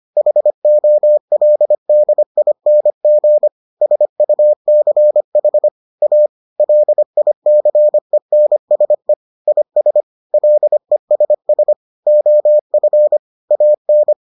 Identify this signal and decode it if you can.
Morse code, heard as a keyed tone, about 25 wpm, 600 Hz. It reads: HOLDING SUCH A LICENSE IS LESS OF AN